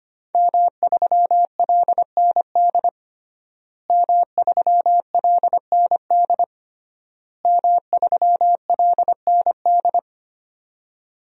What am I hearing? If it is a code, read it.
Morse code, heard as a keyed tone, about 25 wpm, 700 Hz. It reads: M3LND M3LND M3LND
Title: DXCC-100 Call Signs - 5 Characters - Repeated 3x - 25wpm